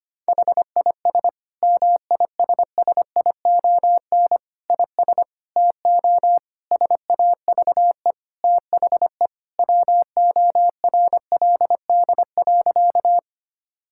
Morse code, heard as a keyed tone, about 25 wpm, 700 Hz. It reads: HIS MISSION IS TO SAVE THE WORLD.
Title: Sentences from Top 1600 Words 25wpm